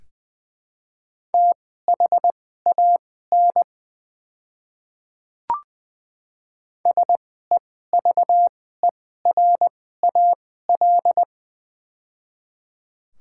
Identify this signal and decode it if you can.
Morse code, heard as a keyed tone, about 20 wpm, spaced as for 14 wpm, 700 Hz. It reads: THAN SEVERAL
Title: 2x Character Spacing - Top 500 Words - 20wpm